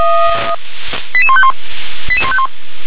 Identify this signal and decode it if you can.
Five-tone selective call (ZVEI), heard as quick runs of five tones, tones of about 70 ms. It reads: 88151 88151
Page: SDR/ZVEI